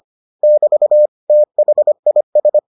Morse code, heard as a keyed tone, about 25 wpm, 600 Hz. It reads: <BT> THIS